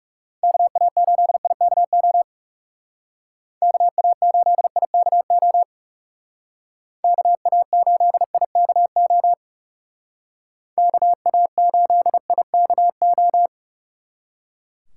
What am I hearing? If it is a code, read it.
Morse code, heard as a keyed tone, about 40 wpm, 700 Hz. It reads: KA8IKO KA8IKO KA8IKO KA8IKO